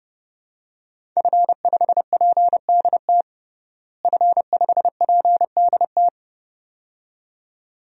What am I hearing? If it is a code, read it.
Morse code, heard as a keyed tone, about 32 wpm, 700 Hz. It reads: F5PDT F5PDT
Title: DXCC-100 Call Signs - 5 Characters - Repeated 2x - 30wpm